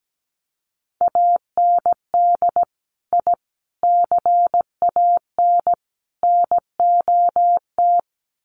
Morse code, heard as a keyed tone, about 17 wpm, 700 Hz. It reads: AND I CAN NOT
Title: Sets of 4 Words 17wpm